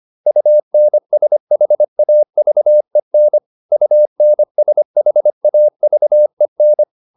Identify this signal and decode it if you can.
Morse code, heard as a keyed tone, about 25 wpm, 600 Hz. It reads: UNSHAVEN UNSHAVEN